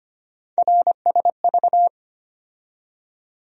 Morse code, heard as a keyed tone, about 25 wpm, 700 Hz. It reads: RSV
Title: USA Call Sign Postfixes - 3 Letters 25wpm